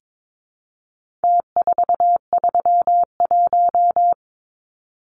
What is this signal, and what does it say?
Morse code, heard as a keyed tone, about 22 wpm, 700 Hz. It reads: T431
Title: Numbers - Contest Style - 4-Digit 22wpm